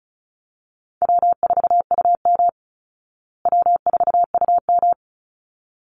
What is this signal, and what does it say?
Morse code, heard as a keyed tone, about 34 wpm, 700 Hz. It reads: W4UM W4UM
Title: Easy Call Signs Repeated 2x - 35wpm